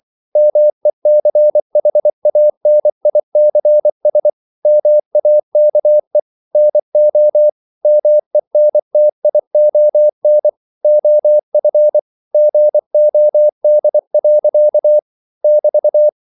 Morse code, heard as a keyed tone, about 24 wpm, 600 Hz. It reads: MECHANICS MAKE NO MENTION OF GOD. <BT>
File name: CW / Practice / EnglishQuotations0424